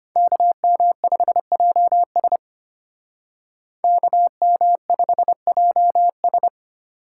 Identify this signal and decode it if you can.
Morse code, heard as a keyed tone, about 32 wpm, 700 Hz. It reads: KM5JS KM5JS